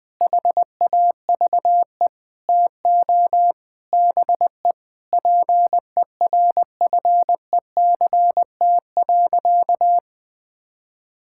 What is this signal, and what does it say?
Morse code, heard as a keyed tone, about 20 wpm, 700 Hz. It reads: HAVE TO BE PERFECT.
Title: Sentences with 5-Word Phrases - Spoken Beginning - 20wpm